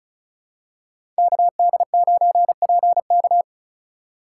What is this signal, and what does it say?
Morse code, heard as a keyed tone, about 34 wpm, 700 Hz. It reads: KD9PK
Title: USA Call Signs - 5 Characters - 6x Character Spacing - 35wpm